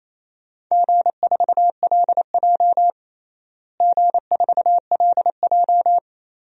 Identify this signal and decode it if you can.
Morse code, heard as a keyed tone, about 28 wpm, 700 Hz. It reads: G4LJ G4LJ
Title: DXCC-100 Call Signs - 4 Characters - Repeated 3x - 28wpm